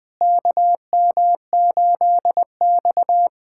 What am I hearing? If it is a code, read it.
Morse code, heard as a keyed tone, about 20 wpm, 700 Hz. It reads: KM8X
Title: USA Callsign DE Callsign - 4 Characters - 20wpm